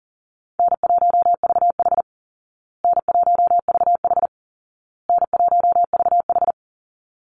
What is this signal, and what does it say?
Morse code, heard as a keyed tone, about 40 wpm, 700 Hz. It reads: N1VH N1VH N1VH
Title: Call Signs Repeated 2x - 40wpm